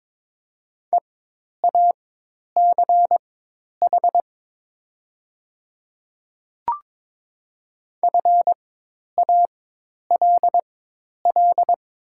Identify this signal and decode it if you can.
Morse code, heard as a keyed tone, about 22 wpm, spaced as for 10 wpm, 700 Hz. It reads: EACH FALL
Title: 4x Character Spacing - Top 500 Words - 22wpm